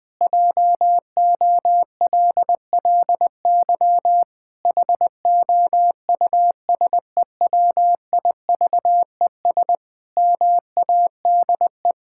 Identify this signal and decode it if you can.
Morse code, heard as a keyed tone, about 20 wpm, 700 Hz. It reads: JOLLY HOUSEWIVES MADE
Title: Pangram Challenge v2 - 20wpm